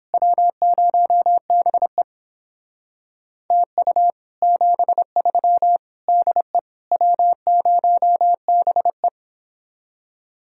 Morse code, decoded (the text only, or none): W0BE TU 73 DE W0BE